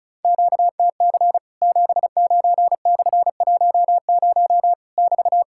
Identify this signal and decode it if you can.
Morse code, heard as a keyed tone, about 34 wpm, 700 Hz. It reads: QTC 79/10 <BT>
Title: Worked All Europe DX Contest - QTC Exchange - 35wpm